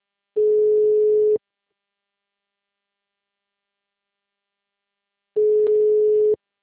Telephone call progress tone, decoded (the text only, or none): ringback tone